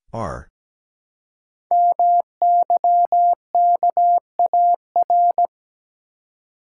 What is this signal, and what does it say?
Morse code, heard as a keyed tone, about 17 wpm, 700 Hz. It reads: MYKAR
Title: Code Groups - 5 Letters - 17wpm